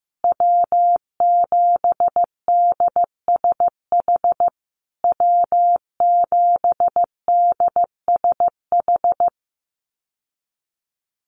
Morse code, heard as a keyed tone, about 15 wpm, 700 Hz. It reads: W7DSH W7DSH
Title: USA Call Signs Repeated 3x 15wpm